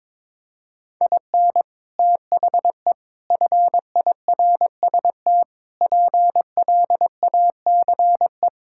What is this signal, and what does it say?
Morse code, heard as a keyed tone, about 21 wpm, 700 Hz. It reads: IN THE FIRST PLACE
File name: practice / 22wpm / Sentences with 4-Word Phrases - 22wpm